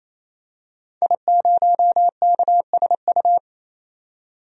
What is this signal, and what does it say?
Morse code, heard as a keyed tone, about 28 wpm, 700 Hz. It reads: I0KSU
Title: DXCC-100 Call Signs - 5 Characters - 28wpm